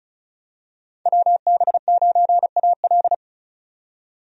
Morse code, heard as a keyed tone, about 36 wpm, 700 Hz. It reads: WB9AL
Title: USA Call Signs - 5 Characters - 35wpm